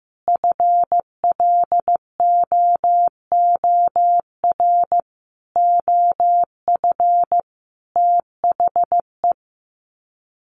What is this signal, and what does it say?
Morse code, heard as a keyed tone, about 15 wpm, 700 Hz. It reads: FLOOR OF THE
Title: Sets of 5 Words 15wpm